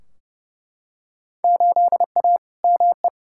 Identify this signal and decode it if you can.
Morse code, heard as a keyed tone, about 30 wpm, 700 Hz. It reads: 8A ME
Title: Field Day Exchanges - 30wpm